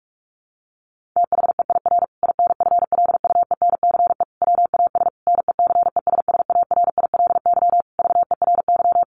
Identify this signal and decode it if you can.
Morse code, heard as a keyed tone, about 44 wpm, 700 Hz. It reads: THEIR INFLUENCE WAS NECESSARILY VERY
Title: Sentences from Top 1400 Words 45wpm